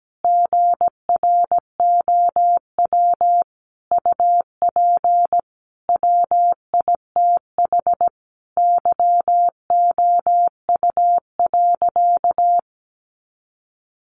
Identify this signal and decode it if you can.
Morse code, heard as a keyed tone, about 17 wpm, 700 Hz. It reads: GROW UP WITH YOU.
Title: Sentences from Top 300 Words 17wpm